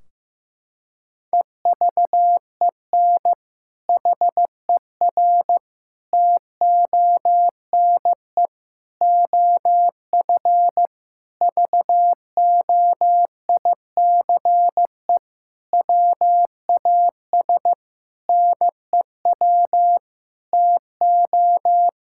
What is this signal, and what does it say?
Morse code, heard as a keyed tone, about 15 wpm, 700 Hz. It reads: EVEN HER TONE OF VOICE WAS NEW TO
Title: Sentences from Top 1600 Words 15wpm